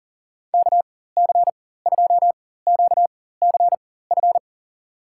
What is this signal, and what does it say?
Morse code, heard as a keyed tone, about 39 wpm, spaced as for 19 wpm, 700 Hz. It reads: KC2QCF
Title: USA Call Signs - 6 Characters - 4x Character Spacing - 40wpm